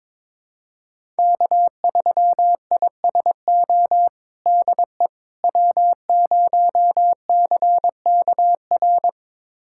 Morse code, heard as a keyed tone, about 22 wpm, 700 Hz. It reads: K3ISO DE W0CKR
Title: USA Callsign DE Callsign - 5 Characters - 22wpm